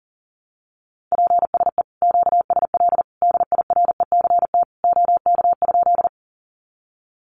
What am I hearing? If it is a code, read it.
Morse code, heard as a keyed tone, about 40 wpm, 700 Hz. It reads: PSE QSL DIRECT OK?
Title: Common QSO Phrases 40wpm